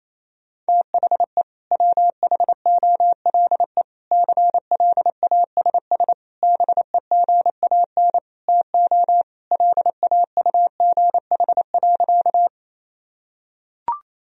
Morse code, heard as a keyed tone, about 28 wpm, 700 Hz. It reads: THE WHOLE CLASS BEGAN TO LAUGH.
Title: Sentences from Top 1000 Words - Review Part 2 - 28wpm